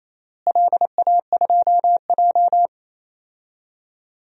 Morse code, heard as a keyed tone, about 28 wpm, 700 Hz. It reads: LA2J
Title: DXCC-100 Call Signs - 4 Characters - Repeated 3x - 28wpm